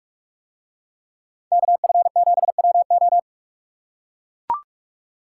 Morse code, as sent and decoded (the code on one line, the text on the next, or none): -.- ..-. --... .-- ---
KF7WO